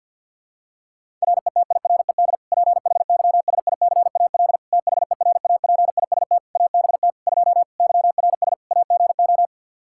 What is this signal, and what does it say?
Morse code, heard as a keyed tone, about 50 wpm, 700 Hz. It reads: RETIRED PHYSICAL THERAPIST ABT 2 YRS AGO
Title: K1USN SST QSO Practice 50wpm